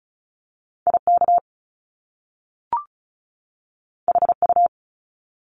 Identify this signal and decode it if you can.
Morse code, heard as a keyed tone, about 36 wpm, 700 Hz. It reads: IK HU
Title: Top Two Letter Combinations in Words 35wpm